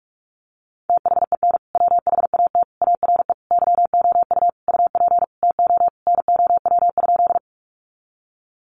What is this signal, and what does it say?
Morse code, heard as a keyed tone, about 44 wpm, 700 Hz. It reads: THEN WHAT ARE YOU UP TO NOW?
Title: Sentences from Top 100 Words 45wpm